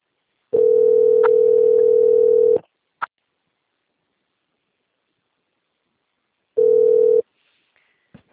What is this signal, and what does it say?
Ringback tone (North America), 440 + 480 Hz.